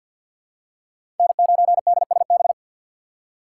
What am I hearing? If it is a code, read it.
Morse code, heard as a keyed tone, about 51 wpm, 700 Hz. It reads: N9DSB
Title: USA Call Signs - 5 Characters - 4x Character Spacing - 50wpm